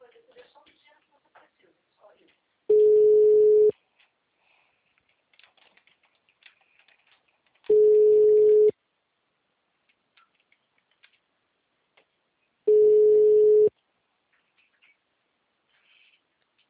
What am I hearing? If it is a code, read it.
Ringback tone (Europe, 425 Hz), with speech.